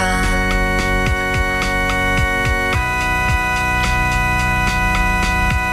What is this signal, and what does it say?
Dial-up modem handshake.